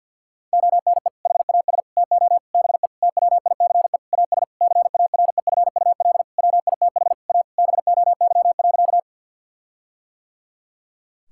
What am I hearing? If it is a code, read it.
Morse code, heard as a keyed tone, about 51 wpm, 700 Hz. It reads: ONE HAS TO BE TWICE AS CAREFUL WITH A BOY.
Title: Sentences from Top 1600 Words 50wpm